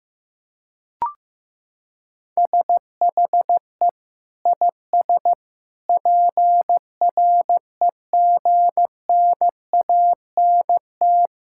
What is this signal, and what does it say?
Morse code, heard as a keyed tone, about 15 wpm, 700 Hz. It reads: SHE IS PREGNANT
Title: Sentences from Top 1000 Words - Review Part 6 - 15wpm